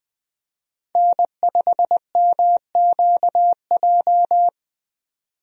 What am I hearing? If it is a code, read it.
Morse code, heard as a keyed tone, about 21 wpm, 700 Hz. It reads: N5MQJ
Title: USA Call Signs - 5 Characters - 20wpm